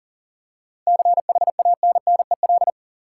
Morse code, heard as a keyed tone, about 38 wpm, 700 Hz. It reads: CHANNEL
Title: Seven Letter Words - 40wpm